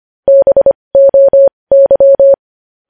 Morse code, heard as a keyed tone, about 25 wpm, spaced as for 20 wpm, 550 Hz. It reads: BOY
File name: podcast / News headlines-1x-25x20